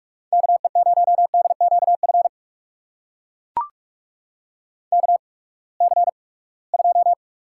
Morse code, heard as a keyed tone, about 43 wpm, spaced as for 25 wpm, 700 Hz. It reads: KE0DQF KC2
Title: USA Call Signs - 6 Characters - 8x Character Spacing - 45wpm